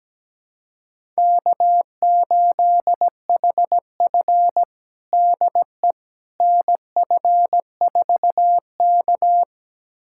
Morse code, heard as a keyed tone, about 17 wpm, 700 Hz. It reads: K8HF DE NF4K
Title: USA Callsign DE Callsign - 4 Characters - 17wpm